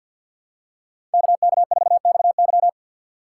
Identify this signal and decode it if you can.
Morse code, heard as a keyed tone, about 52 wpm, 700 Hz. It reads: KK4XY
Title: USA Call Signs - 5 Characters - 3x Character Spacing - 50wpm